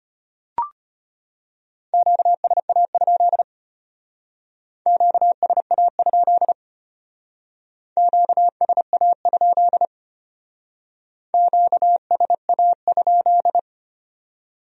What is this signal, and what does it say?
Morse code, heard as a keyed tone, about 32 wpm, 700 Hz. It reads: QSA? QSA? QSA? QSA?